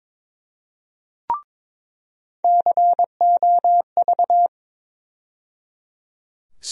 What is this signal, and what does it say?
Morse code, heard as a keyed tone, about 22 wpm, 700 Hz. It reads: COV
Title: USA Call Sign Postfixes - 3 Letters 22wpm